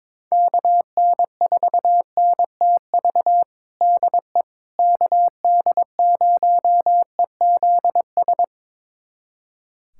Morse code, heard as a keyed tone, about 22 wpm, 700 Hz. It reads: KN4NTV DE KD0EZS
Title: USA Callsign DE Callsign 22wpm